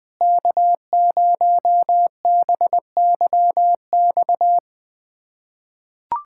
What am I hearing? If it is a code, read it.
Morse code, heard as a keyed tone, about 20 wpm, 700 Hz. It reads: K0BYX
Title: USA Callsign DE Callsign - 5 Characters - 20wpm